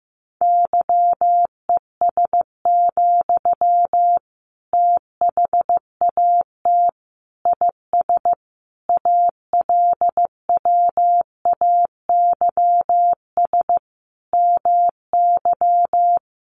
Morse code, heard as a keyed tone, about 15 wpm, 700 Hz. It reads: YES, THAT IS ALWAYS MY
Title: Sentences from Top 1500 Words 15wpm